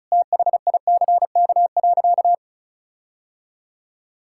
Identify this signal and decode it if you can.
Morse code, heard as a keyed tone, about 36 wpm, 700 Hz. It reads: THICK.
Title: Sentences from Top 1600 Words 35wpm